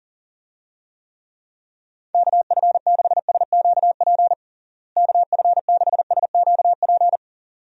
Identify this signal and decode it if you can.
Morse code, heard as a keyed tone, about 40 wpm, 700 Hz. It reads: KF6SQP KF6SQP
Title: USA Call Signs - 6 Characters - Repeated 2x - 40wpm